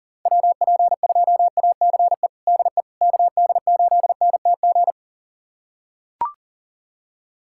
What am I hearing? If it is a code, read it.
Morse code, heard as a keyed tone, about 40 wpm, 700 Hz. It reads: WP2ACE DE KD8NTG
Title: USA Callsign DE Callsign - 6 Characters - 40wpm